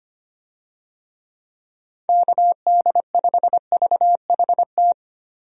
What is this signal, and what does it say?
Morse code, heard as a keyed tone, about 25 wpm, 700 Hz. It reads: KD5VHT
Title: USA Call Signs - 6 Characters - 25wpm